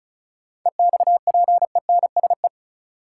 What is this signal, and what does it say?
Morse code, heard as a keyed tone, about 36 wpm, 700 Hz. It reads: EXPENSE
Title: Sets of 4 Words - New Word Drill - 35wpm